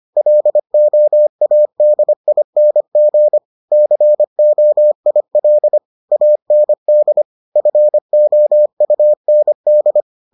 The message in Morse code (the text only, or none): LOADING COIL AND FOUND